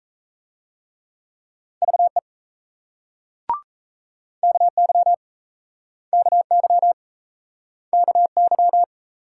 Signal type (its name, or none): Morse code